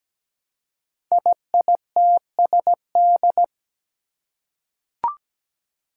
Morse code, heard as a keyed tone, about 17 wpm, 700 Hz. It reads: IITSD
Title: Code Groups - 5 Letters - 17wpm